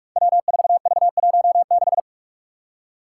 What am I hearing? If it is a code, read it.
Morse code, heard as a keyed tone, about 43 wpm, 700 Hz. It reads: W4V16
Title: Five Challenging Letter-Number Combinations 45wpm